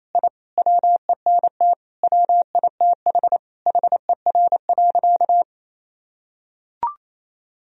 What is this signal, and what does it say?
Morse code, heard as a keyed tone, about 28 wpm, 700 Hz. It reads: I WENT WITH HER.
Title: Sentences from Top 1000 Words - Review Part 3 - 28wpm